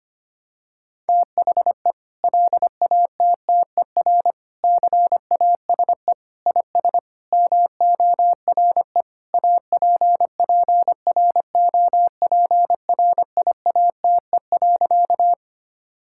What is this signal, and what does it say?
Morse code, heard as a keyed tone, about 25 wpm, 700 Hz. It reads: THE LATTER CASE IS MORE APPROPRIATE.
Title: Sentences from Top 1600 Words 25wpm